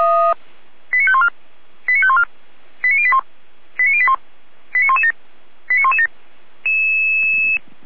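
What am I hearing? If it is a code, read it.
Five-tone selective call (ZVEI), heard as quick runs of five tones, tones of about 70 ms. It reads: 89514 89514 89081 89081 89117 89117
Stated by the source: ZVEI five-tone selective call received on an SDR.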